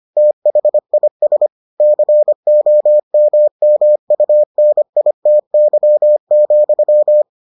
Morse code, heard as a keyed tone, about 25 wpm, 600 Hz. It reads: THIS COMMUNITY,